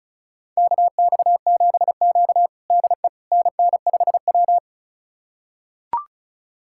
Morse code, heard as a keyed tone, about 36 wpm, 700 Hz. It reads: KX7Q DE NN5W
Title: USA Callsign DE Callsign - 4 Characters - 35wpm